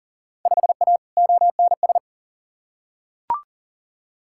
Morse code, heard as a keyed tone, about 40 wpm, 700 Hz. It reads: 5A ONS
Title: Field Day Exchanges Encore - 40wpm